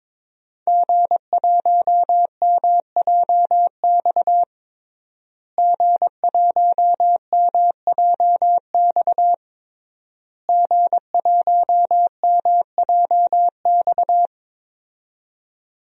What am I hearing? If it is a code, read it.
Morse code, heard as a keyed tone, about 22 wpm, 700 Hz. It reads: G1MJX G1MJX G1MJX
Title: DXCC-100 Call Signs - 5 Characters - Repeated 3x - 22wpm